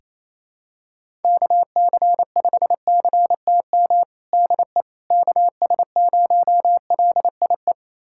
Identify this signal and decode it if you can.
Morse code, heard as a keyed tone, about 28 wpm, 700 Hz. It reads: KC5CTM DE KS0LIE
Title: USA Callsign DE Callsign - 6 Characters - 28wpm